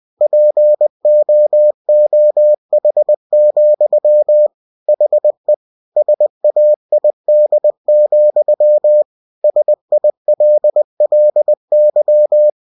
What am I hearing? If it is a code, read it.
Morse code, heard as a keyed tone, about 20 wpm, 600 Hz. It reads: POOH, HE SAID, SILLY
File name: cw-books / winnie-the-pooh / winnie-20wpm-page-0017